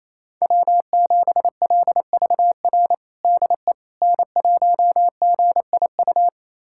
Morse code, heard as a keyed tone, about 28 wpm, 700 Hz. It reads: W7LVR DE N1GIU
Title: USA Callsign DE Callsign - 5 Characters - 28wpm